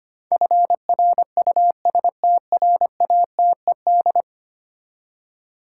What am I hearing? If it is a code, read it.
Morse code, heard as a keyed tone, about 25 wpm, 700 Hz. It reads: FRUSTRATED
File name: 10 Letter Words - First 3 Letters Given - 25wpm